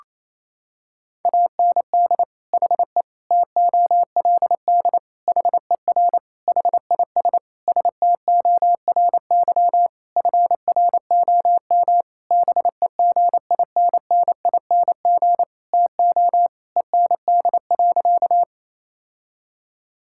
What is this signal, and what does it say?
Morse code, heard as a keyed tone, about 27 wpm, 700 Hz. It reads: AND HE TOLD HER HIS STORY FROM BEGINNING TO END.
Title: Sentences from Top 1000 Words - Review Part 5 - 28wpm